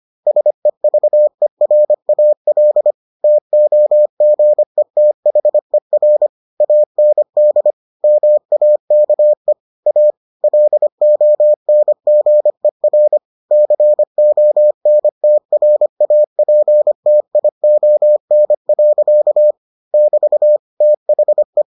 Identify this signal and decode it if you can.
Morse code, heard as a keyed tone, about 25 wpm, 600 Hz. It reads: SEVERAL TOGETHER AND MAKE A LONGER CONTRAPTION. <BT> THE